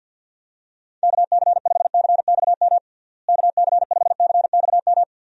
Morse code, heard as a keyed tone, about 52 wpm, 700 Hz. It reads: KC5CXM KC5CXM
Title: USA Call Signs - 6 Characters - Repeated 3x - 50wpm